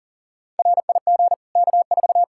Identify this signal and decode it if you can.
Morse code, heard as a keyed tone, about 40 wpm, 700 Hz. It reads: RIG K4
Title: Practice QSO - Exchange 2 - Info - 40wpm